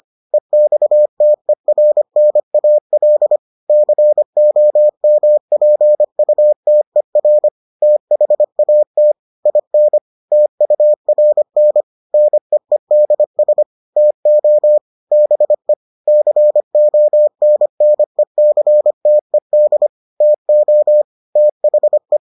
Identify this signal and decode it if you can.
Morse code, heard as a keyed tone, about 25 wpm, 600 Hz. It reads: EXTERNAL COMPUTER THAT IN TURN NEEDS TO BE CONNECTED TO THE